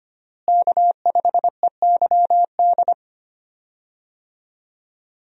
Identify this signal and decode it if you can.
Morse code, heard as a keyed tone, about 25 wpm, 700 Hz. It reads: K5EYD
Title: USA Call Signs - 5 Characters - Repeated 2x - 25wpm